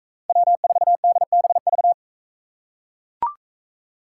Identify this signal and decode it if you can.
Morse code, heard as a keyed tone, about 43 wpm, 700 Hz. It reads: W4DBV